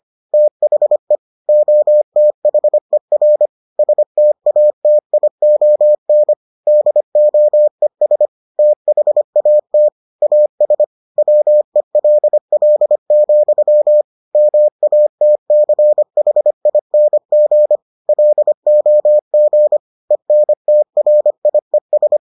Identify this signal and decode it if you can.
Morse code, heard as a keyed tone, about 25 wpm, 600 Hz. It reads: THE OTHER STATION DOES THAT AS WELL, MATCHING LOG ENTRIES